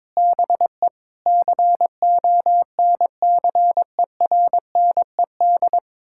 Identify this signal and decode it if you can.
Morse code, heard as a keyed tone, about 22 wpm, 700 Hz. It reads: BE CONCERNED
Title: Sentences from Top 1000 Words - Review Part 6 - 22wpm